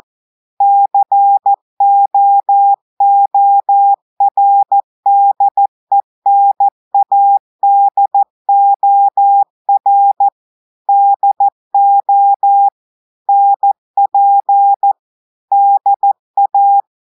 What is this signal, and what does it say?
Morse code, heard as a keyed tone, about 14 wpm, 800 Hz. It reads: COORDENADOR DO NP DA
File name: audios / portugais / portugais texte en clair 1 vitesse 14 mots minutes